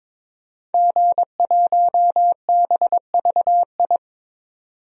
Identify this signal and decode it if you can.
Morse code, heard as a keyed tone, about 22 wpm, 700 Hz. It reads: G1BVI
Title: DXCC-100 Call Signs - 5 Characters - 22wpm